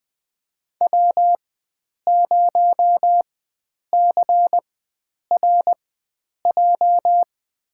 Morse code, heard as a keyed tone, about 20 wpm, spaced as for 9 wpm, 700 Hz. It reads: W0CRJ
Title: USA Call Signs - 5 Characters - 4x Character Spacing - 20wpm